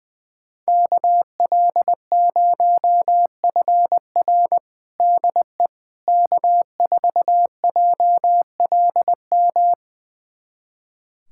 Morse code, heard as a keyed tone, about 20 wpm, 700 Hz. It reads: KL0FR DE K4JLM
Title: USA Callsign DE Callsign - 5 Characters - 20wpm